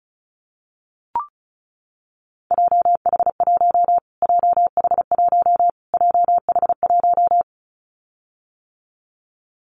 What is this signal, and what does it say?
Morse code, heard as a keyed tone, about 34 wpm, 700 Hz. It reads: JH1 JH1 JH1
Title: Callsign Prefixes for Top 25 DXCC - Repeated 3x - 35wpm